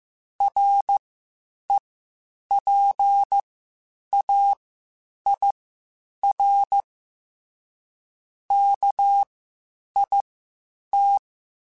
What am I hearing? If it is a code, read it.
Morse code, heard as a keyed tone, about 15 wpm, spaced as for 8 wpm, 800 Hz. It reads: REPAIR KIT
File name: Quotes Spaced / Quotes Spaced-mp3s / Airtight sp @13